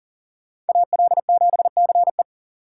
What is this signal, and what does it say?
Morse code, heard as a keyed tone, about 40 wpm, 690 Hz. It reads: AL7CE
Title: USA Call Signs - 5 Characters - 3x Character Spacing - 40wpm